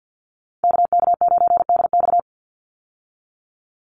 Morse code, heard as a keyed tone, about 50 wpm, 700 Hz. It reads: KK9DX
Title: Call Signs Repeated 2x - 50wpm